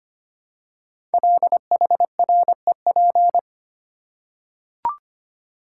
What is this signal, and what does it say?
Morse code, heard as a keyed tone, about 25 wpm, 700 Hz. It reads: LHREP
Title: Code Groups - 5 Letters - 25wpm